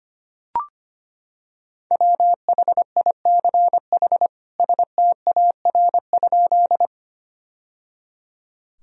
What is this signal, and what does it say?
Morse code, heard as a keyed tone, about 25 wpm, 700 Hz. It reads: WHICH STAR?
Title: Sentences from Top 1000 Words - Review Part 2 - 25wpm